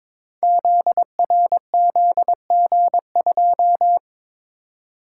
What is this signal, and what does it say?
Morse code, heard as a keyed tone, about 23 wpm, 700 Hz. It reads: ZRZG2
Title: Code Groups - 5 Letter:Numbers - 22wpm